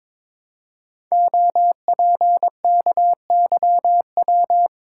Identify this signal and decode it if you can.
Morse code, heard as a keyed tone, about 22 wpm, 700 Hz. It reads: OPKYW
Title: Code Groups - 5 Letter:Numbers - 22wpm